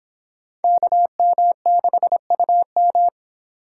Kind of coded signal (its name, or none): Morse code